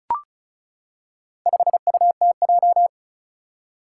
Morse code, heard as a keyed tone, about 36 wpm, 700 Hz. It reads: 5UTJ
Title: Four Challenging Letter-Number Combinations 35wpm